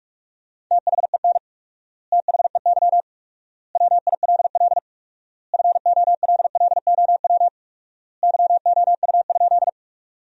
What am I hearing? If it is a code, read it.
Morse code, heard as a keyed tone, about 47 wpm, 700 Hz. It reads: THEN THEY WILL FOLLOW YOU?
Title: 4x Word-Spacing - Sentences from Top 200 Words - 45wpm